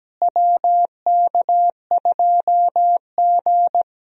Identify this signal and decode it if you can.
Morse code, heard as a keyed tone, about 17 wpm, 700 Hz. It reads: WK2G
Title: USA Call Signs - 4 Characters - Repeated 3x - 17wpm